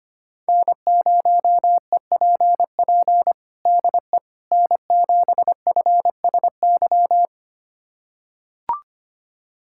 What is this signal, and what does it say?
Morse code, heard as a keyed tone, about 25 wpm, 700 Hz. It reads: N0EPP DE N7FSY
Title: USA Callsign DE Callsign - 5 Characters - 25wpm